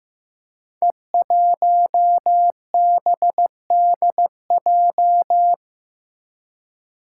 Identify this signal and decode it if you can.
Morse code, heard as a keyed tone, about 15 wpm, 700 Hz. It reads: E1BDJ
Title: Code Groups - 5 Letter:Numbers - 15wpm